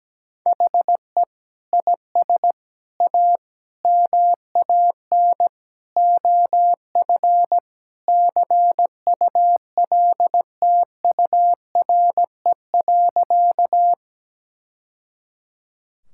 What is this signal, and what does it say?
Morse code, heard as a keyed tone, about 17 wpm, 700 Hz. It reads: HE IS A MAN OF CULTURE.
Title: Sentences from Top 1000 Words - Review Part 6 - 17wpm